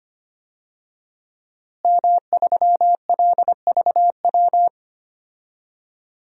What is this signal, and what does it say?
Morse code, heard as a keyed tone, about 25 wpm, 700 Hz. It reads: M3LVW
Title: DXCC-100 Call Signs - 5 Characters - Repeated 3x - 25wpm